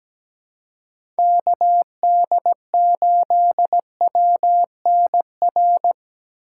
Morse code, heard as a keyed tone, about 17 wpm, 700 Hz. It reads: KD8WNR
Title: USA Call Signs - 6 Characters - 3x Character Spacing - 17wpm